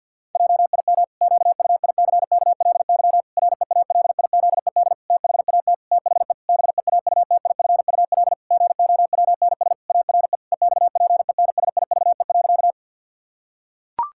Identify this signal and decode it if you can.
Morse code, heard as a keyed tone, about 50 wpm, 700 Hz. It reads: JIM QUICKLY REALIZED THAT THE BEAUTIFUL GOWNS ARE EXPENSIVE.
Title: Pangram Challenge v2 - 50wpm